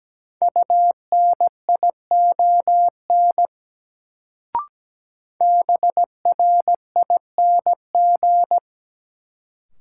Morse code, heard as a keyed tone, about 17 wpm, 700 Hz. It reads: UNION BRING
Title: Sets of 5 Words Encore - New Word Drill - 17wpm